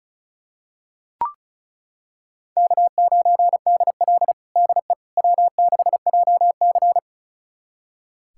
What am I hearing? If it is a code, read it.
Morse code, heard as a keyed tone, about 36 wpm, 700 Hz. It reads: K9DL DE W6JC
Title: USA Callsign DE Callsign - 4 Characters - 35wpm